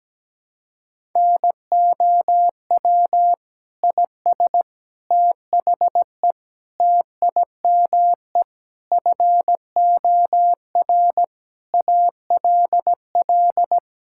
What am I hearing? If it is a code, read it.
Morse code, heard as a keyed tone, about 17 wpm, 700 Hz. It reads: NOW IS THE TIME FOR ALL
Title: Pangram Challenge v2 - 17wpm